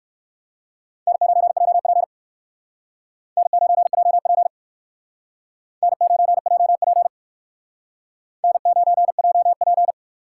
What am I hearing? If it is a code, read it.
Morse code, heard as a keyed tone, about 59 wpm, 700 Hz. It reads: N9JP N9JP N9JP N9JP